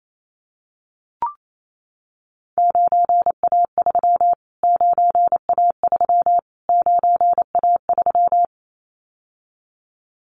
Morse code, heard as a keyed tone, about 28 wpm, 700 Hz. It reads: 9A3 9A3 9A3
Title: Callsign Prefixes for Top 25 DXCC - Repeated 3x - 28wpm